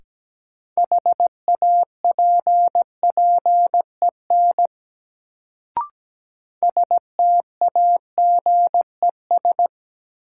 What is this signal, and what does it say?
Morse code, heard as a keyed tone, about 17 wpm, 700 Hz. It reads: HAPPEN STAGES
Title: Sets of 5 Words Encore - New Word Drill - 17wpm